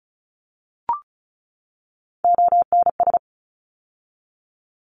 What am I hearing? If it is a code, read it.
Morse code, heard as a keyed tone, about 34 wpm, 700 Hz. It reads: ONS
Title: ARRL Sections 35wpm